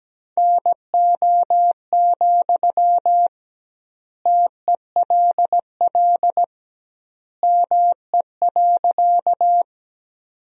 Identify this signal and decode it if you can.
Morse code, heard as a keyed tone, about 17 wpm, 700 Hz. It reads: NO, TELL ME.
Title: 2x Word-Spacing - Sentences from Top 200 Words - 17wpm